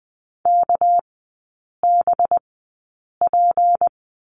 Morse code, heard as a keyed tone, about 20 wpm, spaced as for 8 wpm, 700 Hz. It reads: KBP
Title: 198 Lesson 32e - Letter k - Three Characters 20wpm